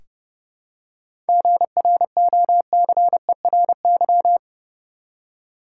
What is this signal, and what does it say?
Morse code, heard as a keyed tone, about 32 wpm, 700 Hz. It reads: GROCERY
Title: Seven Letter Words - 30wpm